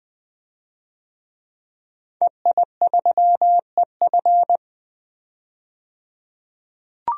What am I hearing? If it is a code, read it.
Morse code, heard as a keyed tone, about 20 wpm, 700 Hz. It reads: EI3EF
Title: DXCC-100 Call Signs - 5 Characters - 20wpm